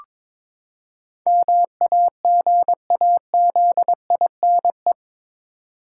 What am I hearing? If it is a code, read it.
Morse code, heard as a keyed tone, about 22 wpm, 700 Hz. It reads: MAGAZINE
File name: practice / 22wpm / Eight Letter Words - 22wpm